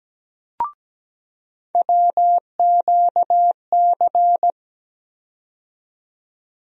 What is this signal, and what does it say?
Morse code, heard as a keyed tone, about 17 wpm, 700 Hz. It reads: WQC
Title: USA Call Sign Postfixes - 3 Letters 17wpm